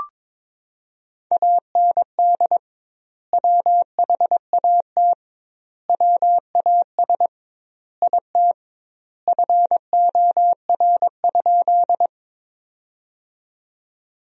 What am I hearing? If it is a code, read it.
Morse code, heard as a keyed tone, about 22 wpm, 700 Hz. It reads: AND WHAT WAS IT FOR?
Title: 2x Word-Spacing - Sentences from Top 100 Words - 22wpm